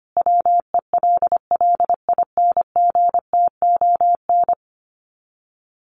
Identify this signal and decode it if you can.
Morse code, heard as a keyed tone, about 25 wpm, 700 Hz. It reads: WELLINGTON
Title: Common QSO Phrases 25wpm